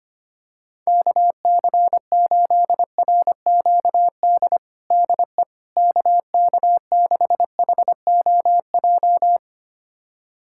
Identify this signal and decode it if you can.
Morse code, heard as a keyed tone, about 25 wpm, 700 Hz. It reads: KC8RQD DE KK6HOJ
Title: USA Callsign DE Callsign - 6 Characters - 25wpm